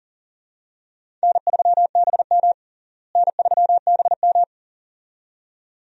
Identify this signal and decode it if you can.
Morse code, heard as a keyed tone, about 40 wpm, 700 Hz. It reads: N3BM N3BM
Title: USA Call Signs - 4 Characters - Repeated 2x - 40wpm